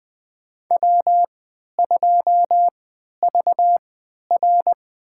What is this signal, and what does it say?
Morse code, heard as a keyed tone, about 21 wpm, spaced as for 12 wpm, 700 Hz. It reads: W2VR
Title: USA Call Signs - 4 Characters - 3x Character Spacing - 20wpm